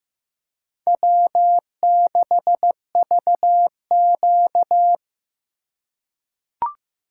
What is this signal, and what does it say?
Morse code, heard as a keyed tone, about 15 wpm, 700 Hz. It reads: W6VQ
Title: USA Call Signs - 4 Characters - 8x Character Spacing - 15wpm